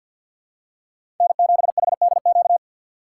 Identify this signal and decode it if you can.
Morse code, heard as a keyed tone, about 48 wpm, 700 Hz. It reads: N7HDQ
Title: USA Call Signs - 5 Characters - 2x Character Spacing - 50wpm